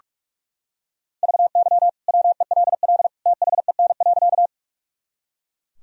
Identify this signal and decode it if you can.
Morse code, heard as a keyed tone, about 41 wpm, 700 Hz. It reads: VY WELL THEN.
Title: Sentences from Top 200 Words and Abbreviations 45wpm